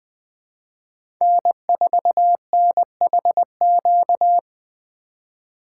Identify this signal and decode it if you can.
Morse code, heard as a keyed tone, about 20 wpm, 700 Hz. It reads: N4NHQ
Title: USA Call Signs - 5 Characters - 3x Character Spacing - 20wpm